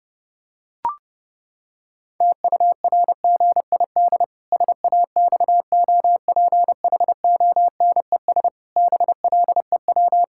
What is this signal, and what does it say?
Morse code, heard as a keyed tone, about 30 wpm, 700 Hz. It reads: TURGID SAXOPHONES BLEW
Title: Pangram Challenge v2 - 30wpm